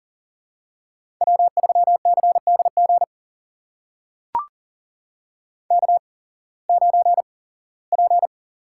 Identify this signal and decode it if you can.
Morse code, heard as a keyed tone, about 42 wpm, spaced as for 21 wpm, 700 Hz. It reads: W3CDG K9P
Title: USA Call Signs - 5 Characters - 8x Character Spacing - 40wpm